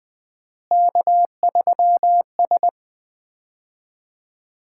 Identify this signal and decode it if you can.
Morse code, heard as a keyed tone, about 20 wpm, 700 Hz. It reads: K3S
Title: Common QSO Phrases - 4x Word-Spacing - 20wpm